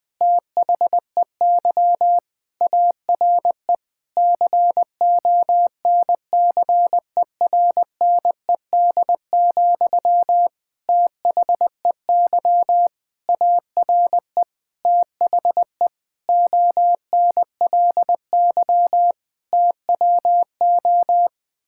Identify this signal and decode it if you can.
Morse code, heard as a keyed tone, about 21 wpm, 700 Hz. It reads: THEY ARE CONCERNED, THEY ARE THE ONLY TWO
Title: Sentences from Top 1000 Words - Review Part 6 - 20wpm